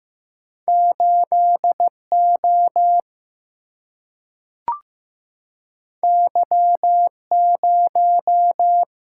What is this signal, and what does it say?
Morse code, heard as a keyed tone, about 15 wpm, 700 Hz. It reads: 8 O Y 0
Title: Single-Letter Single-Number 15wpm